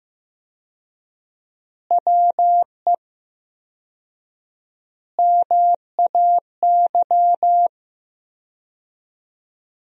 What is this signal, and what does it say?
Morse code, heard as a keyed tone, about 15 wpm, 700 Hz. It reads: WE MAY
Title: 4x Word-Spacing - Sentences from Top 300 Words - 15wpm